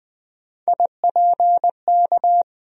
Morse code, heard as a keyed tone, about 20 wpm, 700 Hz. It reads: IPK
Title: USA Call Sign Postfixes - 3 Letters 20wpm